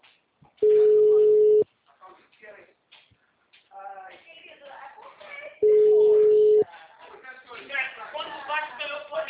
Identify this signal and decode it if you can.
Ringback tone (Europe, 425 Hz), with speech.